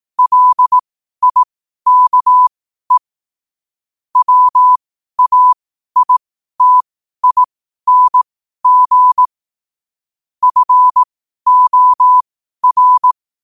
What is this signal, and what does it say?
Morse code, heard as a keyed tone, about 18 wpm, spaced as for 12 wpm, 1000 Hz. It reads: LIKE WAITING FOR
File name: qotdfiles / 12wpm